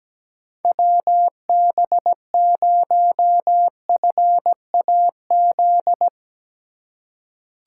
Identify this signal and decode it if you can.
Morse code, heard as a keyed tone, about 17 wpm, 700 Hz. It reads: WB0FAZ
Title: USA Call Signs - 3x Character Spacing - 17wpm